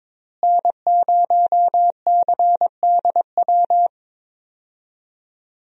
Morse code, heard as a keyed tone, about 22 wpm, 700 Hz. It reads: N0CDW
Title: USA Call Signs - 5 Characters - Repeated 2x - 22wpm